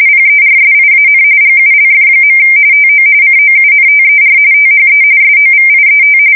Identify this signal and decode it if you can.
RTTY, about 45.45 baud. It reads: PERSIAN GULF. TO COUNTER THE SOVIET
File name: fsk170 / battleshipnj